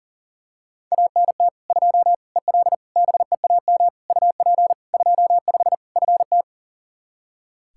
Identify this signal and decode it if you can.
Morse code, heard as a keyed tone, about 40 wpm, 700 Hz. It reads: ANT 2 EL BEAM UP 25 FT
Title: Practice QSO - Exchange 2 - Info - 40wpm